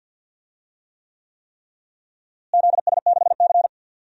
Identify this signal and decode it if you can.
Morse code, heard as a keyed tone, about 48 wpm, 700 Hz. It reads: ZS6C
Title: DXCC-100 Call Signs - 4 Characters - Repeated 3x - 50wpm